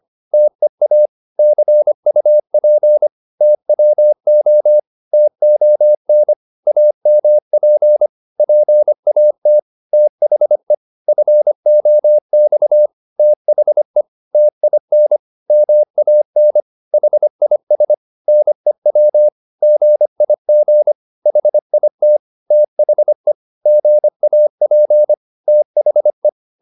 Morse code, heard as a keyed tone, about 25 wpm, 600 Hz. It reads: TEA CUP TWO TON AMP PAT THE FOX THE TIN MAN HIS NEW GIG HIT THE GAP THE
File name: PR204a 25